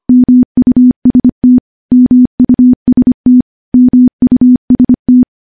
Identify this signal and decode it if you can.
Morse code, heard as a keyed tone, about 25 wpm, 250 Hz. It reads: MUST MUST MUST
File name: morse-training-25WPM